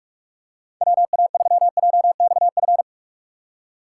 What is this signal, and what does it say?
Morse code, heard as a keyed tone, about 46 wpm, 700 Hz. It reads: WA3JXF
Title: USA Call Signs - 6 Characters - 2x Character Spacing - 45wpm